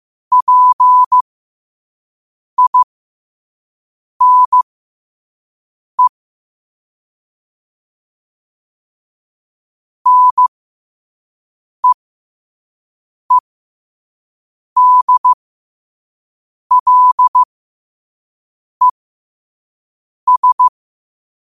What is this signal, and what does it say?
Morse code, heard as a keyed tone, about 15 wpm, spaced as for 5 wpm, 1000 Hz. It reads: PINE NEEDLES
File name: qotdfiles / 5wpm